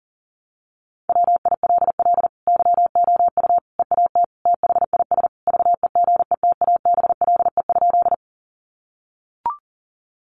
Morse code, heard as a keyed tone, about 40 wpm, 700 Hz. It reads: WILL YOU EAT THIS VEGETABLE?
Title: Sentences from Top 1300 Words 40wpm